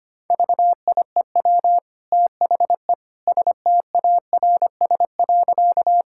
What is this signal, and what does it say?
Morse code, heard as a keyed tone, about 25 wpm, 700 Hz. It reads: VIEW THE STARS.
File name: Easy Sentences with Hints - 25wpm